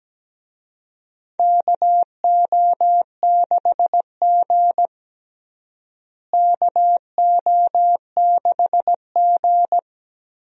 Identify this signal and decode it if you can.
Morse code, heard as a keyed tone, about 17 wpm, 700 Hz. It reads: KO6G KO6G
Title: USA Call Signs - 4 Characters - Repeated 3x - 17wpm